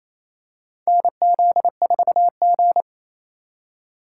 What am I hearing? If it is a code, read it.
Morse code, heard as a keyed tone, about 28 wpm, 700 Hz. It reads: NZ4G
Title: USA Call Signs - 4 Characters - 3x Character Spacing - 28wpm